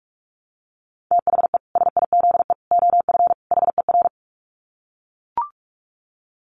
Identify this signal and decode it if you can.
Morse code, heard as a keyed tone, about 44 wpm, 700 Hz. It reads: THE SIZE OF HER
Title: Dit Challenge 45wpm